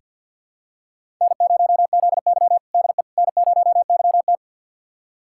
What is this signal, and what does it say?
Morse code, heard as a keyed tone, about 46 wpm, 700 Hz. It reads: N0ZY DE N0YT
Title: USA Callsign DE Callsign - 4 Characters - 50wpm